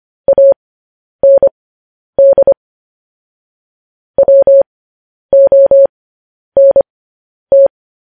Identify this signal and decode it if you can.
Morse code, heard as a keyed tone, about 25 wpm, spaced as for 10 wpm, 550 Hz. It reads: AND WONT
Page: news headlines in morse code at 10 wpm